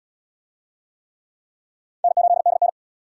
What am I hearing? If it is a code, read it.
Morse code, heard as a keyed tone, about 66 wpm, 700 Hz. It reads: N8DN